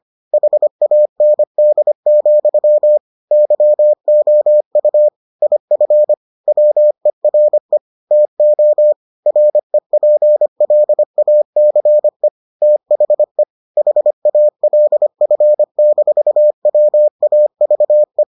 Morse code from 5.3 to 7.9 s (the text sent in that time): IF WERE